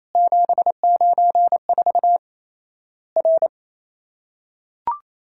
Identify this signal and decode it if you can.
Morse code, heard as a keyed tone, about 27 wpm, 700 Hz. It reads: 794 R
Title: Worked All Europe DX Contest - QTC Exchange - 28wpm